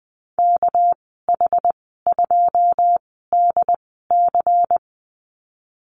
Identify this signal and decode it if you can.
Morse code, heard as a keyed tone, about 20 wpm, spaced as for 14 wpm, 700 Hz. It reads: KH2DC
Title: 207 Lesson 33g - Letter j - Call Signs 20wpm